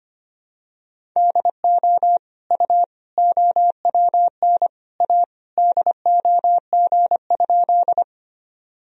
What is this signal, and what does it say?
Morse code, heard as a keyed tone, about 25 wpm, 700 Hz. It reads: DO U OWN A DOG?
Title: Sentences from Top 400 Words and Abbreviations 25wpm